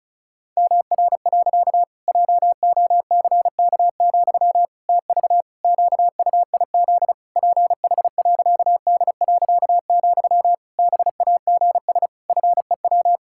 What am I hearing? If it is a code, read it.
Morse code, heard as a keyed tone, about 36 wpm, 700 Hz. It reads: MR. JOCK, TV QUIZ PH.D., BAGS FEW
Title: Pangram Challenge v2 - 35wpm